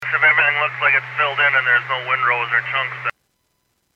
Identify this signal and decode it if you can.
Emergency Alert System (SAME) tones, heard as harsh buzzing bursts, with speech. It is heard from 0.0 to 3.0 s.